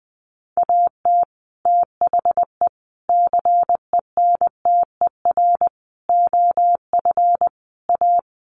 Morse code, heard as a keyed tone, about 20 wpm, 700 Hz. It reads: AT THE CENTER OF A
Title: Sets of 5 Words Encore2 20wpm